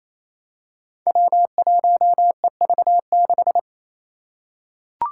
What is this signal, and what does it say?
Morse code, heard as a keyed tone, about 28 wpm, 700 Hz. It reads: W1EV6
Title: Five Challenging Letter-Number Combinations 28wpm